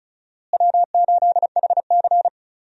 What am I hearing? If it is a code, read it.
Morse code, heard as a keyed tone, about 36 wpm, 700 Hz. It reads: W8HC
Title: USA Call Signs - 4 Characters - Repeated 2x - 35wpm